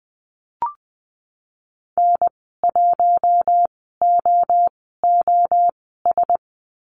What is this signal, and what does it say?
Morse code, heard as a keyed tone, about 20 wpm, spaced as for 14 wpm, 700 Hz. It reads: N1OOS
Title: 048 Lesson 9g - Number 4 - Call Signs 20wpm